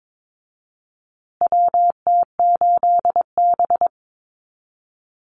Morse code, heard as a keyed tone, about 22 wpm, 700 Hz. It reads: WT8B
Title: USA Call Signs Repeated 3x 22wpm